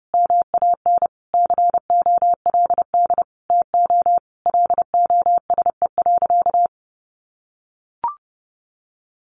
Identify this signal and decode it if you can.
Morse code, heard as a keyed tone, about 30 wpm, 700 Hz. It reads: MAN COLD TO LOSE.
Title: Sentences from Top 1100 Words 30wpm